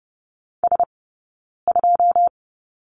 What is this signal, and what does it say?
Morse code, heard as a keyed tone, about 30 wpm, spaced as for 9 wpm, 700 Hz. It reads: S2
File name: Lesson 18d - Letter m - Two Characters 30wpm